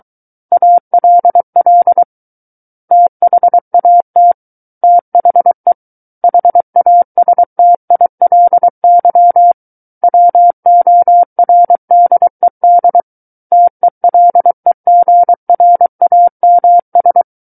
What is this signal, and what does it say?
Morse code, heard as a keyed tone, about 23 wpm, 700 Hz. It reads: ALL THAT THE HASTILY WORDED TELEGRAMS
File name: Book One / 14-Chapter 14 - 23 WPM